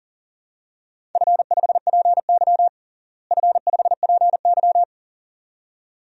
Morse code, heard as a keyed tone, about 40 wpm, 700 Hz. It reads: F5PY F5PY
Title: DXCC-100 Call Signs - 4 Characters - Repeated 2x - 40wpm